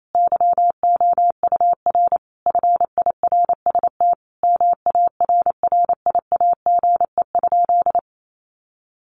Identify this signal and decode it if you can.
Morse code, heard as a keyed tone, about 28 wpm, 700 Hz. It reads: YOUR FIRST MARRIAGE?
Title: Sentences from Top 1500 Words 28wpm